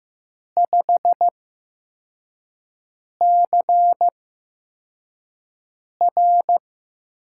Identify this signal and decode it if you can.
Morse code, heard as a keyed tone, about 15 wpm, spaced as for 4 wpm, 700 Hz. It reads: 5CR
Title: USA Call Signs - 5 Characters - 8x Character Spacing - 15wpm